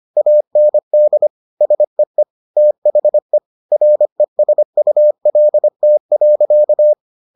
Morse code, heard as a keyed tone, about 25 wpm, 600 Hz. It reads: AND SEE THE RESULT.